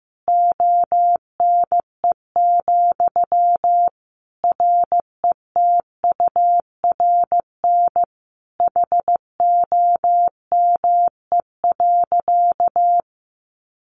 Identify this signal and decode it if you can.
Morse code, heard as a keyed tone, about 15 wpm, 700 Hz. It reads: ONE, RETURN HOME.
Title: Sentences from Top 800 Words 15wpm